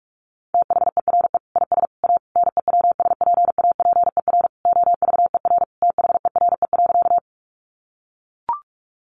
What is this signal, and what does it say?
Morse code, heard as a keyed tone, about 44 wpm, 700 Hz. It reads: THERE IS A NEWSPAPER OVER THERE.
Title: Sentences from Top 1100 Words 45wpm